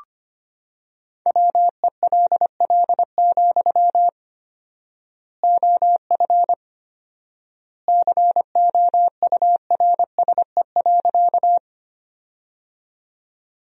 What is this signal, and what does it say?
Morse code, heard as a keyed tone, about 25 wpm, 700 Hz. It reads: WELL, OF COURSE.
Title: 4x Word-Spacing - Sentences from Top 500 Words - 25wpm